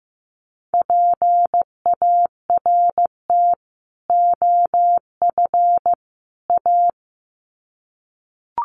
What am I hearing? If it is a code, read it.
Morse code, heard as a keyed tone, about 15 wpm, 700 Hz. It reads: PART OF A
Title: Sets of 3 Words 15wpm